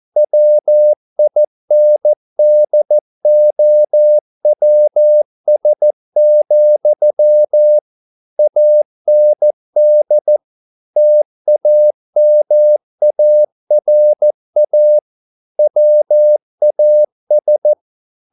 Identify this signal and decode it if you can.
Morse code, heard as a keyed tone, about 14 wpm, 600 Hz. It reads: WINDOWS, AND TAMARA WAS